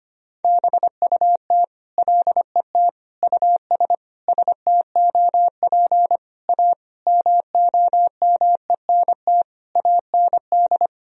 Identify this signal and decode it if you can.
Morse code, heard as a keyed tone, about 25 wpm, 700 Hz. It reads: BUT LET US STOP A MOMENT AND
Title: Sentences from Top 1000 Words - Review Part 1 - 25wpm